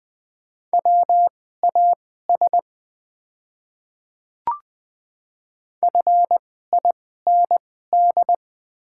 Morse code, heard as a keyed tone, about 20 wpm, spaced as for 14 wpm, 700 Hz. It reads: WAS FIND
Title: 2x Character Spacing - Top 100 Words - 20wpm